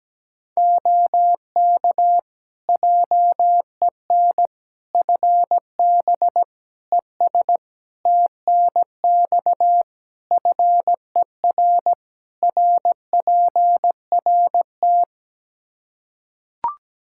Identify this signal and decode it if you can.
Morse code, heard as a keyed tone, about 17 wpm, 700 Hz. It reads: OK JEN FB ES TNX FER RPRT
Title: Practice QSO - Exchange 2 - Info - 17wpm